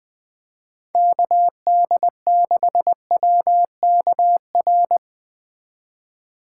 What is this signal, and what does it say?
Morse code, heard as a keyed tone, about 20 wpm, 700 Hz. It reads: KD6WKR